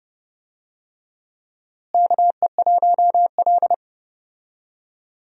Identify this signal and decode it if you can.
Morse code, heard as a keyed tone, about 30 wpm, 700 Hz. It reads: KE1L
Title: USA Call Signs - 4 Characters - Repeated 2x - 30wpm